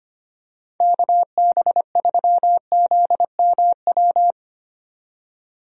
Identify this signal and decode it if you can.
Morse code, heard as a keyed tone, about 25 wpm, 700 Hz. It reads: KB3ZMW
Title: USA Call Signs - 6 Characters - 25wpm